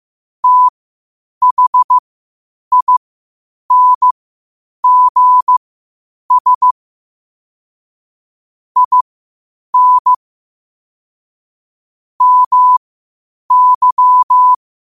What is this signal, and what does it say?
Morse code, heard as a keyed tone, about 15 wpm, spaced as for 8 wpm, 1000 Hz. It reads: THINGS IN MY
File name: qotdfiles / 8wpm